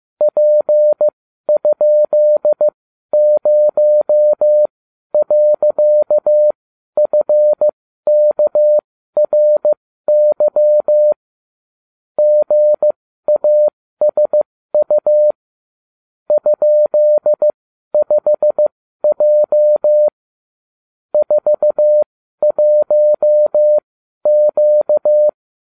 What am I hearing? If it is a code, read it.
Morse code, heard as a keyed tone, about 15 wpm, spaced as for 12 wpm, 600 Hz. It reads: P?0.FKRY GASU ?5J 41Q
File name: ubacw36